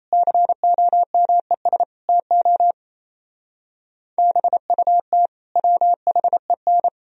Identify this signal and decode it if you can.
Morse code, heard as a keyed tone, about 34 wpm, 700 Hz. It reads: COMES TO BUT WHEN
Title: Speed-Racing - Sets of 5 Words 28wpm